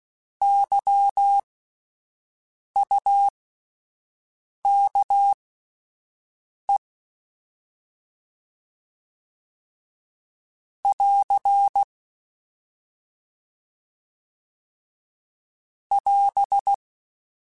Morse code, heard as a keyed tone, about 16 wpm, spaced as for 5 wpm, 800 Hz. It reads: YUKE <AR> <AS>
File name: Alphabetical / Alphabetical mp3s / Ys @5